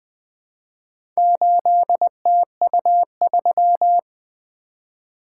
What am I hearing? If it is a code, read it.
Morse code, heard as a keyed tone, about 20 wpm, 700 Hz. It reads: 8TU3
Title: Four Letter-Number Combinations - 20wpm